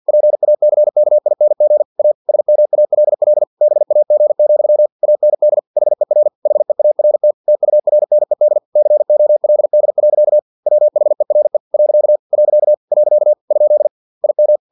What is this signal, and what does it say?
Morse code, heard as a keyed tone, about 50 wpm, 600 Hz. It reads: PACKING A SMALL BAG, AND HER HEART TURNED COLD. WHERE . . ? IM
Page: Never 49wpm